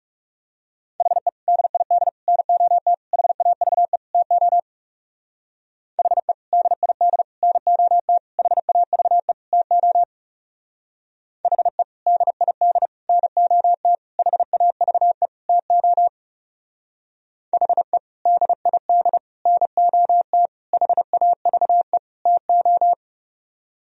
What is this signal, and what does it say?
Morse code, heard as a keyed tone, about 39 wpm, 700 Hz. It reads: HE DID NOT HAVE TO HE DID NOT HAVE TO HE DID NOT HAVE TO HE DID NOT HAVE TO